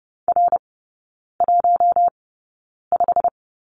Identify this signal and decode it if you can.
Morse code, heard as a keyed tone, about 30 wpm, spaced as for 9 wpm, 700 Hz. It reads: R15
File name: course / 30:10wpm / 087 Lesson 15e - Number 5 - Three Characters 30wpm